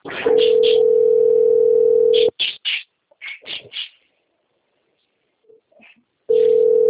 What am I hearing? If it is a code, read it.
Ringback tone (North America), 440 + 480 Hz.